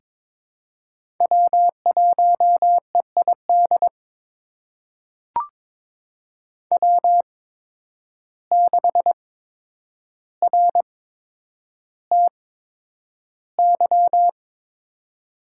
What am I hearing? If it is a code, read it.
Morse code, heard as a keyed tone, about 23 wpm, spaced as for 10 wpm, 700 Hz. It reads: W1EID W6RTY
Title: USA Call Signs - 5 Characters - 8x Character Spacing - 22wpm